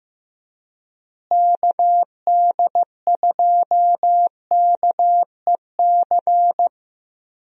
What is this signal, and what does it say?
Morse code, heard as a keyed tone, about 15 wpm, 700 Hz. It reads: KD2KEC
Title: USA Call Signs - 6 Characters - Repeated 2x - 15wpm